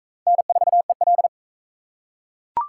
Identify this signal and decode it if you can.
Morse code, heard as a keyed tone, about 40 wpm, 700 Hz. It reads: N4EL